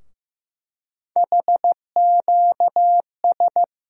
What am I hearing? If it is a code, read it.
Morse code, heard as a keyed tone, about 15 wpm, 700 Hz. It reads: HQS
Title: USA Call Sign Postfixes - 3 Letters 15wpm